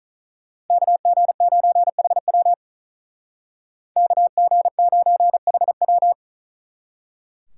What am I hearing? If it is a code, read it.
Morse code, heard as a keyed tone, about 41 wpm, 700 Hz. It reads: KG9HW KG9HW
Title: Speed Racing - USA 2x2 Call Signs 35wpm